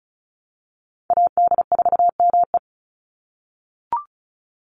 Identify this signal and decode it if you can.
Morse code, heard as a keyed tone, about 34 wpm, 700 Hz. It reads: AD4ME
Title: USA Call Signs 35wpm